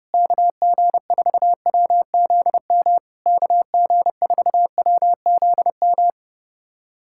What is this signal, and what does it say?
Morse code, heard as a keyed tone, about 30 wpm, 700 Hz. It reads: KG4WZM KG4WZM
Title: SOTA QSO Practice 30wpm